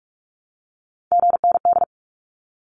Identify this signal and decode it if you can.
Morse code, heard as a keyed tone, about 44 wpm, 700 Hz. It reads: GND
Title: Abbreviations 45wpm